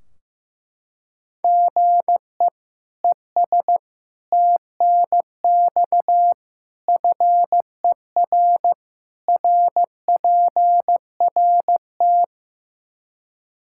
Morse code, heard as a keyed tone, about 15 wpm, 700 Hz. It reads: GE ES TNX FER RPRT
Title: Practice QSO - Exchange 1 - Beginning - 15wpm